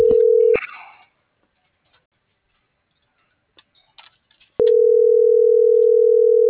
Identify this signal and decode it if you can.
Ringback tone (North America), 440 + 480 Hz.